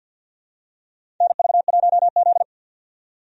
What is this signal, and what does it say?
Morse code, heard as a keyed tone, about 51 wpm, 700 Hz. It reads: NV1Z
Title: USA Call Signs - 4 Characters - 50wpm